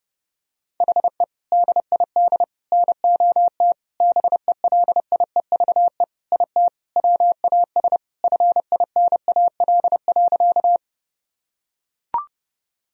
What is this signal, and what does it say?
Morse code, heard as a keyed tone, about 30 wpm, 700 Hz. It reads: HE DID NOT BELIEVE IT WAS FINAL.
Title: Sentences from Top 1000 Words - Review Part 2 - 30wpm